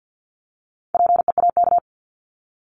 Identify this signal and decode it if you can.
Morse code, heard as a keyed tone, about 50 wpm, 700 Hz. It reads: PEAK
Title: Top 1751-2000 Words 50wpm